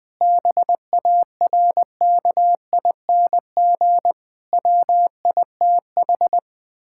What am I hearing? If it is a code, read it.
Morse code, heard as a keyed tone, about 21 wpm, 700 Hz. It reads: BARKING WITH
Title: Pangram Challenge v2 - 20wpm